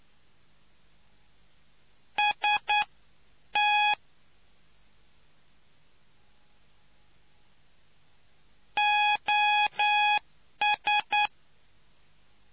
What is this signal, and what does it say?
Morse code, heard as a keyed tone, about 9 wpm, spaced as for 8 wpm, 830 Hz. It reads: ST OS